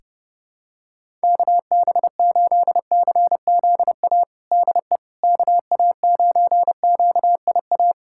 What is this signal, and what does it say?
Morse code, heard as a keyed tone, about 32 wpm, 700 Hz. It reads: KB8CZA DE KA9QIA
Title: USA Callsign DE Callsign 30wpm